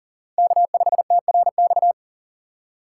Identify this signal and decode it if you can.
Morse code, heard as a keyed tone, about 40 wpm, 700 Hz. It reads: K5TRX
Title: USA Call Signs - 5 Characters - 6x Character Spacing - 40wpm